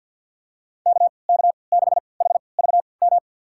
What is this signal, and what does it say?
Morse code, heard as a keyed tone, about 50 wpm, spaced as for 28 wpm, 690 Hz. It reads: KK6HVM
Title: USA Call Signs - 6 Characters - 3x Character Spacing - 50wpm